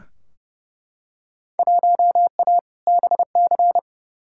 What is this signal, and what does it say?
Morse code, heard as a keyed tone, about 30 wpm, 700 Hz. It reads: 1A BC
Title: Field Day Exchanges - 30wpm